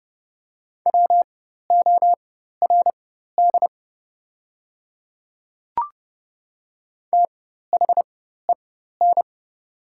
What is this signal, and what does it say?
Morse code, heard as a keyed tone, about 30 wpm, spaced as for 14 wpm, 700 Hz. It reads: WORD THEN
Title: 4x Character Spacing - Top 100 Words - 30wpm